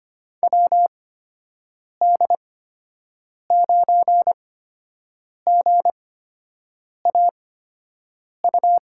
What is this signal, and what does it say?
Morse code, heard as a keyed tone, about 25 wpm, spaced as for 7 wpm, 700 Hz. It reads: WD9GAU
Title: USA Call Signs - 6 Characters - 8x Character Spacing - 25wpm